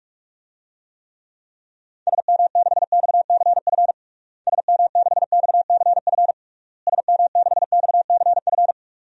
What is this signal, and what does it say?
Morse code, heard as a keyed tone, about 46 wpm, 700 Hz. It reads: SM6XCF SM6XCF SM6XCF
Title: DXCC-100 Call Signs - 6 Characters - Repeated 3x - 45wpm